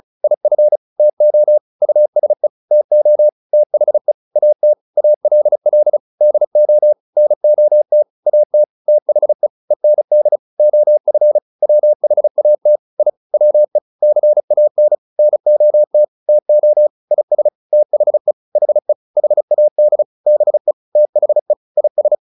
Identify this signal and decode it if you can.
Morse code, heard as a keyed tone, about 34 wpm, 600 Hz. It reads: IF TO USE TO THE AT ALL DO NOT AT THE END OF WHAT I WE CAN NOT TO IS THE HE HAD BE THE IS